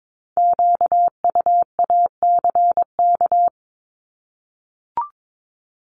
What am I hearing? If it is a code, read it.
Morse code, heard as a keyed tone, about 22 wpm, 700 Hz. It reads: QUACK
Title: Multi-letter Phonemes 22wpm